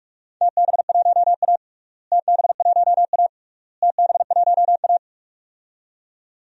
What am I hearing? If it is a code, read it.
Morse code, heard as a keyed tone, about 44 wpm, 700 Hz. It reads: TB1A TB1A TB1A
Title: DXCC-100 Call Signs - 4 Characters - Repeated 3x - 45wpm